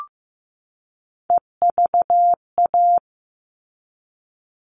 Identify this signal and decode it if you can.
Morse code, heard as a keyed tone, about 15 wpm, 700 Hz. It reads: EVA
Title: All Cut Numbers - 3-Digit 15wpm